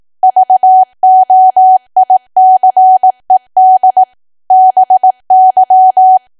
Morse code, heard as a keyed tone, about 18 wpm, 750 Hz. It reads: VOICED BY
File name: Archive / 18 WPM / 18WPM